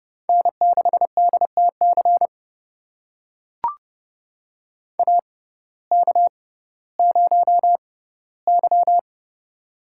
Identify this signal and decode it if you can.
Morse code, heard as a keyed tone, about 30 wpm, spaced as for 17 wpm, 700 Hz. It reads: N6DTC AK0Y
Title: USA Call Signs - 6x Character Spacing - 30wpm